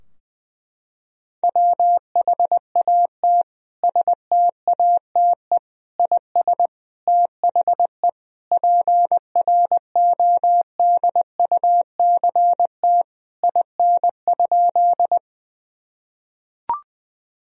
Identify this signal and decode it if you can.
Morse code, heard as a keyed tone, about 21 wpm, 700 Hz. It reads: WHAT STATE IS THE PRODUCT IN?
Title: Sentences from Top 1000 Words - Review Part 2 - 20wpm